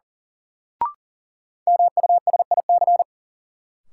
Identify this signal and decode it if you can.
Morse code, heard as a keyed tone, about 37 wpm, 700 Hz. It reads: MUSIC
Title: Sentences from Top 300 Words - New Word Drill - 40wpm